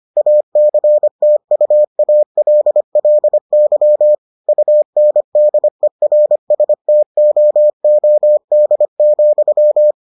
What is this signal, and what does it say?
Morse code, heard as a keyed tone, about 25 wpm, 600 Hz. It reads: ACTUALLY UNDERSTOOD,